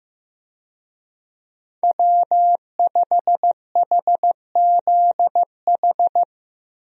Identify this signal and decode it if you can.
Morse code, heard as a keyed tone, about 15 wpm, 700 Hz. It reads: W5HZH
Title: USA Call Signs - 5 Characters - Repeated 3x - 15wpm